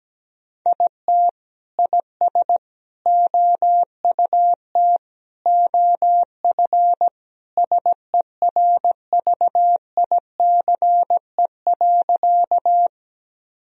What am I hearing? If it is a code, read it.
Morse code, heard as a keyed tone, about 17 wpm, 700 Hz. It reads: IT IS OUT OF SERVICE.
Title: Sentences from Top 1000 Words - Review Part 4 - 17wpm